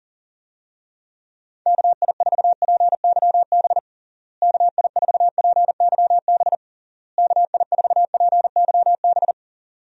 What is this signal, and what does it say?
Morse code, heard as a keyed tone, about 37 wpm, 700 Hz. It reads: KI4PYB KI4PYB KI4PYB
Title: USA Call Signs - 6 Characters - Repeated 3x - 40wpm